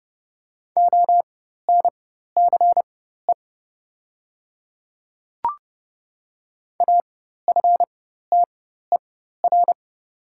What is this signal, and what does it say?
Morse code, heard as a keyed tone, about 30 wpm, spaced as for 14 wpm, 690 Hz. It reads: ONCE AFTER
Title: 4x Character Spacing - Top 500 Words - 30wpm